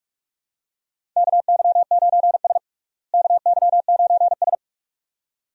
Morse code, heard as a keyed tone, about 46 wpm, 700 Hz. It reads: KY9S KY9S
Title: USA Call Signs - 4 Characters - Repeated 2x - 45wpm